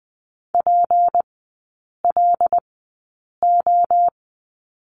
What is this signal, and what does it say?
Morse code, heard as a keyed tone, about 20 wpm, spaced as for 8 wpm, 700 Hz. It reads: PLO
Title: 172 Lesson 28e - Number 9 - Three Characters 20wpm